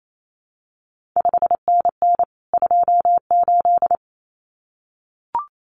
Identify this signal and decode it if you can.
Morse code, heard as a keyed tone, about 28 wpm, 700 Hz. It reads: 5NN 28
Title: CQ WW DX Contest Exchange Encore - 28wpm